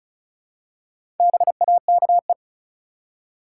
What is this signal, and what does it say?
Morse code, heard as a keyed tone, about 34 wpm, 700 Hz. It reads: BAKE
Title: Four Letter Words - 35wpm